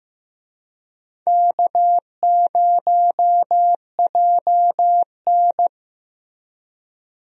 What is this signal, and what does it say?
Morse code, heard as a keyed tone, about 15 wpm, 700 Hz. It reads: K0JN
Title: USA Call Signs - 4 Characters - Repeated 3x - 15wpm